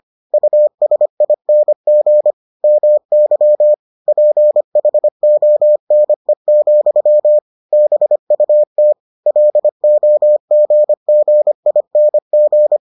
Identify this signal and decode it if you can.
Morse code, heard as a keyed tone, about 25 wpm, 600 Hz. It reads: USING MY PHONE, BUT LOGGING